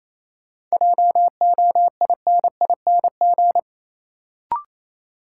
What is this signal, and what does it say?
Morse code, heard as a keyed tone, about 28 wpm, 700 Hz. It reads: JOINING
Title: Sets of 5 Words - New Word Drill - 28wpm